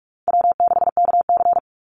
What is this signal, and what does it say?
Morse code, heard as a keyed tone, about 44 wpm, 700 Hz. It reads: W6KC
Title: Call Signs - 45wpm